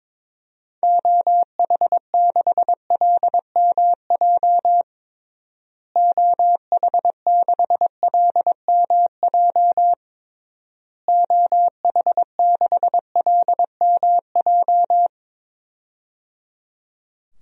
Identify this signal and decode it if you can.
Morse code, heard as a keyed tone, about 22 wpm, 700 Hz. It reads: OH6LMJ OH6LMJ OH6LMJ
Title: DXCC-100 Call Signs - 6 Characters - Repeated 3x - 22wpm